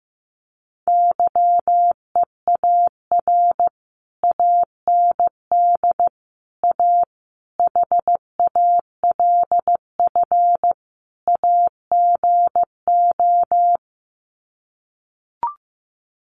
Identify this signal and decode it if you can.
Morse code, heard as a keyed tone, about 15 wpm, 700 Hz. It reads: YEAR AND A HALF AGO
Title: Sets of 5 Words Encore 15wpm